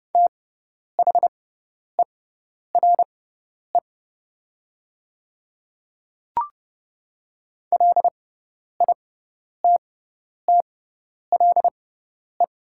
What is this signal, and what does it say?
Morse code, heard as a keyed tone, about 31 wpm, spaced as for 10 wpm, 700 Hz. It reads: THERE LITTLE
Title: 6x Character Spacing - Top 500 Words - 30wpm